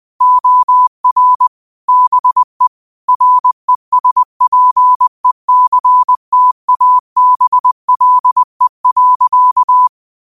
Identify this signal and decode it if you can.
Morse code, heard as a keyed tone, about 20 wpm, 1000 Hz. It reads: OR BE RESPECTABLE.
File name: qotdfiles / 20wpm